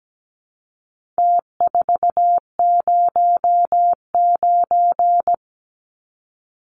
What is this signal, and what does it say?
Morse code, heard as a keyed tone, about 17 wpm, 700 Hz. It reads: T409
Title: Numbers - Contest Style - 4-Digit 17wpm